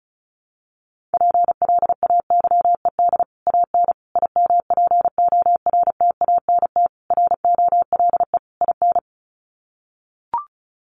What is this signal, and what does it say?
Morse code, heard as a keyed tone, about 34 wpm, 700 Hz. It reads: PLAYED AN IMPORTANT ROLE IN
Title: Sets of 5 Words Encore2 35wpm